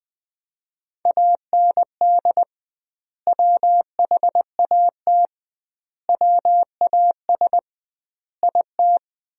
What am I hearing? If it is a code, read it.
Morse code, heard as a keyed tone, about 20 wpm, 700 Hz. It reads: AND WHAT WAS IT
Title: 2x Word-Spacing - Sentences from Top 100 Words - 20wpm